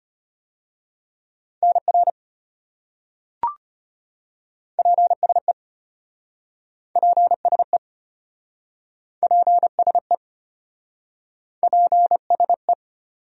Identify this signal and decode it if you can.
Morse code, heard as a keyed tone, about 33 wpm, 700 Hz. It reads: NR PSE PSE PSE PSE